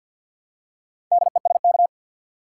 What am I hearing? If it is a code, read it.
Morse code, heard as a keyed tone, about 50 wpm, 700 Hz. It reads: DESK
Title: Four Letter Words - 50wpm